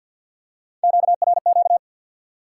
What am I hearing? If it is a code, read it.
Morse code, heard as a keyed tone, about 53 wpm, 700 Hz. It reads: QRQ
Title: Top 90 QSO Elements 50wpm